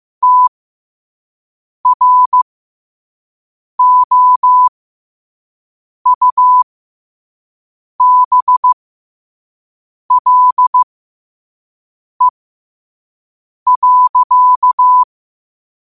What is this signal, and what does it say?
Morse code, heard as a keyed tone, about 15 wpm, spaced as for 5 wpm, 1000 Hz. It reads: TROUBLE.
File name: qotdfiles / 5wpm